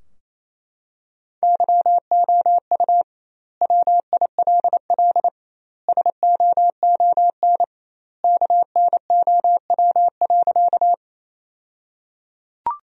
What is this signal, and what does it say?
Morse code, heard as a keyed tone, about 28 wpm, 700 Hz. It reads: YOU WILL SOON KNOW.
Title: 2x Word-Spacing - Sentences from Top 400 Words - 28wpm